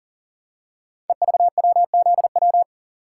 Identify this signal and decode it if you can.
Morse code, heard as a keyed tone, about 40 wpm, 700 Hz. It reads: EVWZW
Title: Code Groups - 5 Letter:Numbers - 40wpm